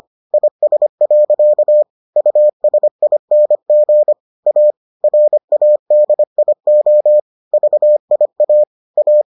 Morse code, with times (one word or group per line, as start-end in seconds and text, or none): 0.3-1.8 IS.
2.2-4.1 USING
4.5-4.7 A
5.0-7.2 RADIO
7.5-8.6 VIA
9.0-9.2 A